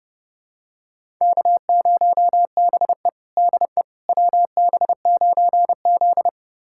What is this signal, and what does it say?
Morse code, heard as a keyed tone, about 30 wpm, 700 Hz. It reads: K0BE DE WB9Z
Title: USA Callsign DE Callsign - 4 Characters - 30wpm